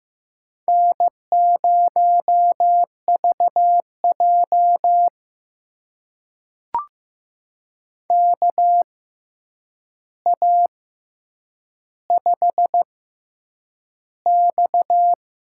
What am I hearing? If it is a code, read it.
Morse code, heard as a keyed tone, about 15 wpm, spaced as for 8 wpm, 700 Hz. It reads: N0VJ KA5X
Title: USA Call Signs - 4 Characters - 6x Character Spacing - 15wpm